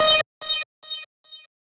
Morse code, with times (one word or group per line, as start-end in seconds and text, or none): none